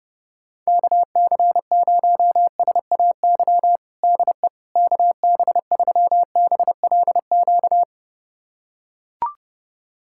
Morse code, heard as a keyed tone, about 32 wpm, 700 Hz. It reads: KC0SAY DE KB3BLQ
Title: USA Callsign DE Callsign - 6 Characters - 30wpm